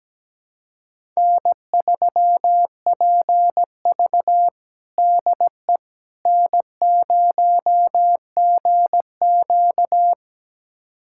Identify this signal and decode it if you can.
Morse code, heard as a keyed tone, about 17 wpm, 700 Hz. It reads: N3PV DE N0GQ
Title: USA Callsign DE Callsign - 4 Characters - 17wpm